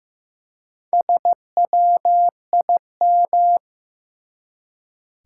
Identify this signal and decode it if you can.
Morse code, heard as a keyed tone, about 15 wpm, 700 Hz. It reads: SWIM
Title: Four Letter Words - 15wpm